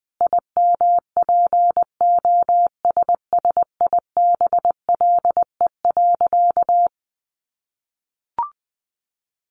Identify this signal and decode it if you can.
Morse code, heard as a keyed tone, about 20 wpm, 700 Hz. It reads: IMPOSSIBLE.
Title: Sentences from Top 1100 Words 20wpm